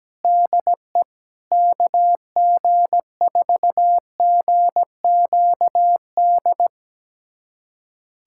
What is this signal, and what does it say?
Morse code, heard as a keyed tone, about 17 wpm, 700 Hz. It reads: DE KG4GQD
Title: USA Callsign DE Callsign - 6 Characters - 17wpm